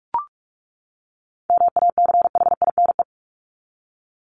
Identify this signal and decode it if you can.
Morse code, heard as a keyed tone, about 44 wpm, 700 Hz. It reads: MACHINE
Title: Top 500 Words 45wpm